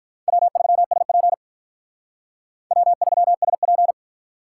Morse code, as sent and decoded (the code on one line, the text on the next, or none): .-- ...-- ... .--. / .-- ...-- ... .--.
W3SP W3SP